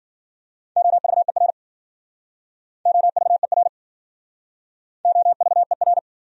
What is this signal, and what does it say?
Morse code, heard as a keyed tone, about 52 wpm, 700 Hz. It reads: OVER OVER OVER